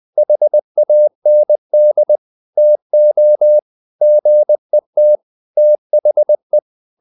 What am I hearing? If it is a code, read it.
Morse code, heard as a keyed tone, about 20 wpm, 600 Hz. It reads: HAND TO GET THE
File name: the old man and the sea part 2 20wpm